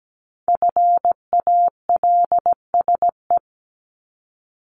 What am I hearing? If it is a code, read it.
Morse code, heard as a keyed tone, about 17 wpm, 700 Hz. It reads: FALSE
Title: Top 1501-1750 Words 17wpm